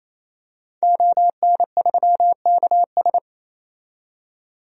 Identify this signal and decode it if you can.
Morse code, heard as a keyed tone, about 28 wpm, 700 Hz. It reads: ON3KS
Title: DXCC-100 Call Signs - 5 Characters - 28wpm